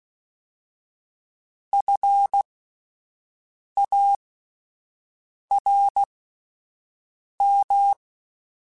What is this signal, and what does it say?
Morse code, heard as a keyed tone, about 16 wpm, 800 Hz. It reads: F A R M
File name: General / General-mp3s / Houses @10